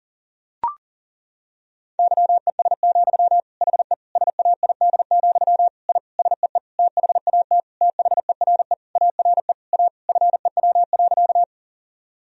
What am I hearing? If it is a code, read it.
Morse code, heard as a keyed tone, about 40 wpm, 700 Hz. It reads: YES, HE SAID, I SEE THAT THERE ARE A FEW.
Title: Sentences from Top 1000 Words - Review Part 1 - 40wpm